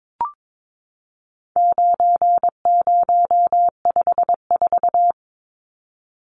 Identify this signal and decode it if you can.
Morse code, heard as a keyed tone, about 22 wpm, 700 Hz. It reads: 9054
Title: Numbers - 4-Digit 22wpm